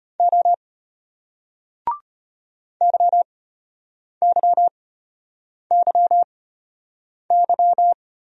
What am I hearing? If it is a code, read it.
Morse code, heard as a keyed tone, about 34 wpm, spaced as for 8 wpm, 700 Hz. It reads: O YYYY